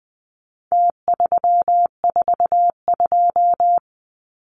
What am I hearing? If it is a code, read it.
Morse code, heard as a keyed tone, about 20 wpm, 700 Hz. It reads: T342
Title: Numbers - Contest Style - 4-Digit 20wpm